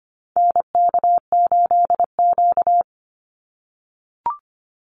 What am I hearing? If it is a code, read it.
Morse code, heard as a keyed tone, about 25 wpm, 700 Hz. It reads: NK8Q
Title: NA SOTA Activator Callsigns 25wpm